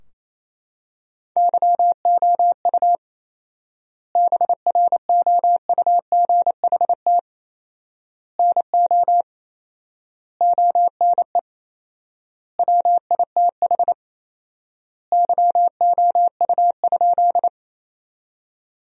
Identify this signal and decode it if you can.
Morse code, heard as a keyed tone, about 28 wpm, 700 Hz. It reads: YOU BROUGHT NO ONE WITH YOU?
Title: 4x Word-Spacing - Sentences from Top 500 Words - 28wpm